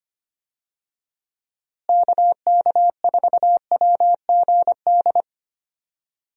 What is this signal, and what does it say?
Morse code, heard as a keyed tone, about 25 wpm, 700 Hz. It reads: KK4WGD
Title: USA Call Signs - 6 Characters - Repeated 3x - 25wpm